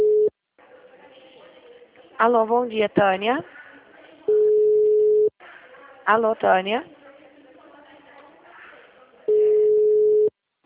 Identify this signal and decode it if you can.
Ringback tone (Europe, 425 Hz), with speech.